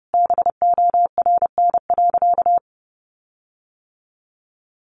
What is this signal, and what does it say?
Morse code, heard as a keyed tone, about 30 wpm, 700 Hz. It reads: BORN.
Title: Sentences from Top 1300 Words 30wpm